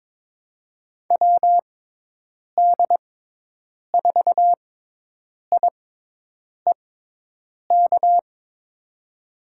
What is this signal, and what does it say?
Morse code, heard as a keyed tone, about 23 wpm, spaced as for 8 wpm, 700 Hz. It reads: WD4IEK
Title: USA Call Signs - 6x Character Spacing - 22wpm